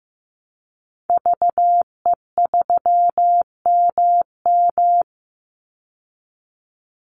Morse code, heard as a keyed tone, about 14 wpm, 700 Hz. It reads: VE3MM
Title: Call Signs Repeated 2x - 15wpm